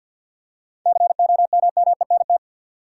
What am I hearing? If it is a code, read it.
Morse code, heard as a keyed tone, about 50 wpm, 700 Hz. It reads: COMMENT
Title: Seven Letter Words - 50wpm